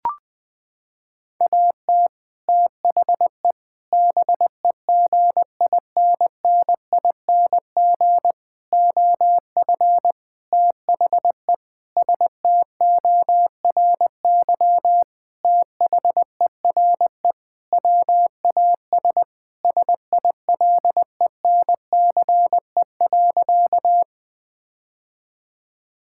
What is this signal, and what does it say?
Morse code, heard as a keyed tone, about 20 wpm, 700 Hz. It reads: AT THE BEGINNING OF THE STORY THERE WAS SILENCE.
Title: Sentences with 5-Word Phrases - 20wpm